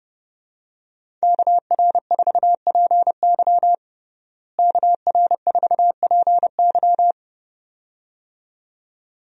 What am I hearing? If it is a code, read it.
Morse code, heard as a keyed tone, about 30 wpm, 700 Hz. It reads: KR4PY KR4PY
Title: USA Call Signs - 5 Characters - Repeated 2x - 30wpm